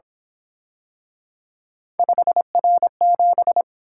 Morse code, heard as a keyed tone, about 27 wpm, 700 Hz. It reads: 5R7
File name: Speed-Racing - Three Letter-Number Combinations - 22wpm